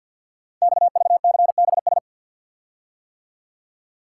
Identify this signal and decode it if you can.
Morse code, heard as a keyed tone, about 50 wpm, 690 Hz. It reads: XVCBS
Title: Code Groups - 5 Letters - 50wpm